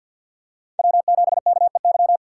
Morse code, heard as a keyed tone, about 52 wpm, 700 Hz. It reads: W7KEY
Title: USA Call Signs - 2x Character Spacing - 50wpm